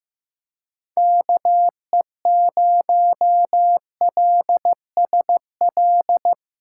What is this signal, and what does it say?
Morse code, heard as a keyed tone, about 15 wpm, 700 Hz. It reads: KE0LSL
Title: USA Callsign DE Callsign 15wpm